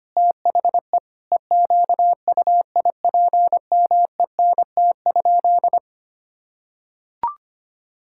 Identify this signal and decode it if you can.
Morse code, heard as a keyed tone, about 25 wpm, 700 Hz. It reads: THE EQUIPMENT?
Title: Sentences from Top 1000 Words - Review Part 5 - 25wpm